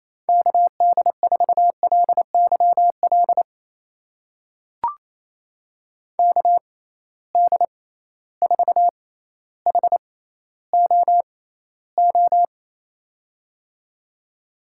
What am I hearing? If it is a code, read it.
Morse code, heard as a keyed tone, about 28 wpm, spaced as for 14 wpm, 700 Hz. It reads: KD4LYL KD4HOO
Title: USA Call Signs - 6 Characters - 6x Character Spacing - 28wpm